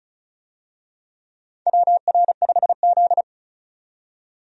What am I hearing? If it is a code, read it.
Morse code, heard as a keyed tone, about 36 wpm, 700 Hz. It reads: WR5Z
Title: USA Call Signs - 4 Characters - 35wpm